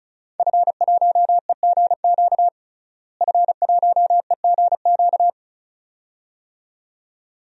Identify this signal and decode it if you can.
Morse code, heard as a keyed tone, about 36 wpm, 700 Hz. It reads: F1EGQ F1EGQ
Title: DXCC-100 Call Signs - 5 Characters - Repeated 2x - 35wpm